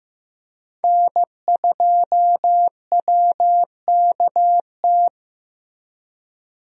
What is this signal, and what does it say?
Morse code, heard as a keyed tone, about 15 wpm, 700 Hz. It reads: N2WKT
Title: USA Call Signs - 5 Characters - Repeated 3x - 15wpm